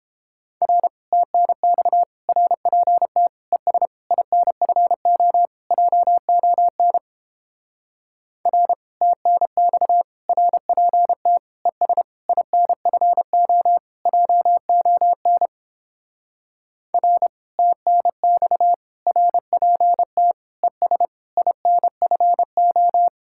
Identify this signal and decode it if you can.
Morse code, heard as a keyed tone, about 31 wpm, 700 Hz. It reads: R TNX RPT ES INFO JON R TNX RPT ES INFO JON R TNX RPT ES INFO